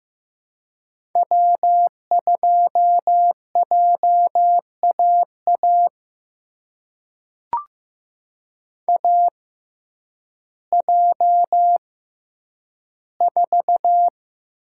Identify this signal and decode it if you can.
Morse code, heard as a keyed tone, about 15 wpm, spaced as for 9 wpm, 700 Hz. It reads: W2JAA AJ4
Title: USA Call Signs - 5 Characters - 6x Character Spacing - 15wpm